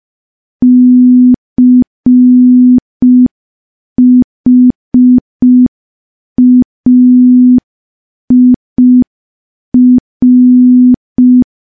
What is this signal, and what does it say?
Morse code, heard as a keyed tone, about 5 wpm, 250 Hz. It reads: CHAIR
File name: morse / morse-training-05WPM